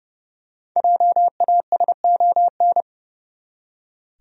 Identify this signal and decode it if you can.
Morse code, heard as a keyed tone, about 30 wpm, 700 Hz. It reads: JASON
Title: USA Common Names - 30wpm